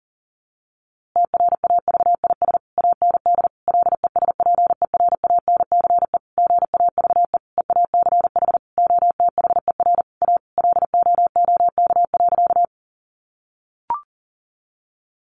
Morse code, heard as a keyed tone, about 40 wpm, 700 Hz. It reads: TRAVIS AND LESPERANCE GAVE EACH OTHER A LOOK.
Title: A Sound Of Thunder 40wpm